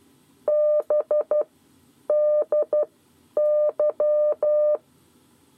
Morse code, heard as a keyed tone, about 12 wpm, spaced as for 9 wpm, 600 Hz. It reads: BDY
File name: 7 Lezione cw 2